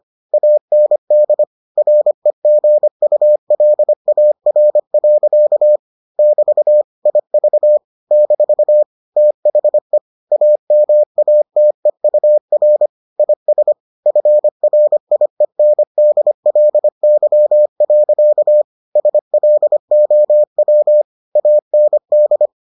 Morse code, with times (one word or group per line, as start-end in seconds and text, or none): 0.3-1.4 AND
1.8-5.8 REGULAR.
6.2-6.8 <BT>
7.1-7.8 IV
8.1-8.8 -
9.2-10.0 THE
10.3-12.9 AMATEUR
13.2-13.7 IS
14.1-18.6 FRIENDLY.
19.0-21.0 SLOW
21.4-22.5 AND